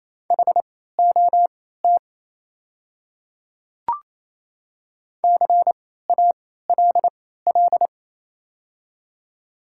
Morse code, heard as a keyed tone, about 28 wpm, spaced as for 16 wpm, 700 Hz. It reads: HOT CALL
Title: 3x Character Spacing - Top 100 Words - 28wpm